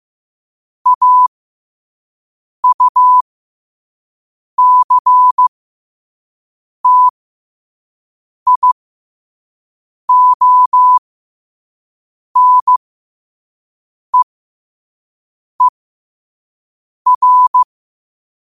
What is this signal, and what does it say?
Morse code, heard as a keyed tone, about 15 wpm, spaced as for 5 wpm, 1000 Hz. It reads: AUCTIONEER
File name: qotdfiles / 5wpm